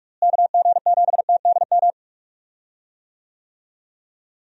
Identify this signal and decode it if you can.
Morse code, heard as a keyed tone, about 46 wpm, 700 Hz. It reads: KG7TDM
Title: USA Call Signs - 6 Characters - Repeated 2x - 45wpm